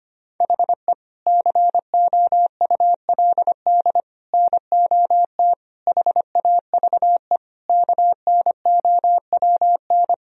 Morse code, heard as a keyed tone, about 25 wpm, 700 Hz. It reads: HE COULD NOT HAVE KNOWN
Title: Sentences from Top 1000 Words - Review Part 3 - 25wpm